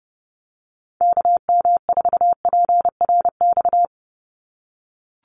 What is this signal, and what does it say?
Morse code, heard as a keyed tone, about 30 wpm, 700 Hz. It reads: KM4PRX
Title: USA Call Signs 30wpm